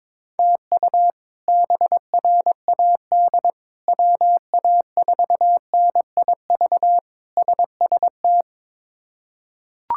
Morse code, heard as a keyed tone, about 22 wpm, 700 Hz. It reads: TU BRAD WA4NIV SST
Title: K1USN SST QSO Practice - 22wpm